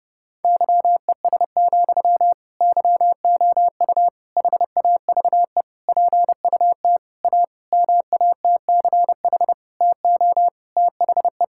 Morse code, heard as a keyed tone, about 30 wpm, 700 Hz. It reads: YES, YOU HAVE PUT A MATCH TO THE
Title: Sentences from Top 1600 Words 30wpm